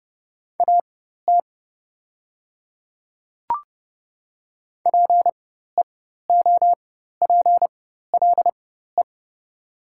Morse code, heard as a keyed tone, about 30 wpm, spaced as for 14 wpm, 700 Hz. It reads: AT PEOPLE
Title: 4x Character Spacing - Top 100 Words - 30wpm